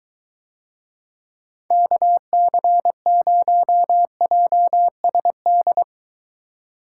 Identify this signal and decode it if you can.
Morse code, heard as a keyed tone, about 21 wpm, 700 Hz. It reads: KC0JSD